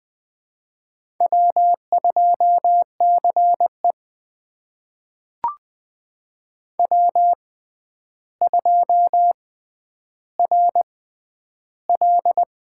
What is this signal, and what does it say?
Morse code, heard as a keyed tone, about 20 wpm, spaced as for 10 wpm, 700 Hz. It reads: W2CE W2RL
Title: USA Call Signs - 4 Characters - 6x Character Spacing - 20wpm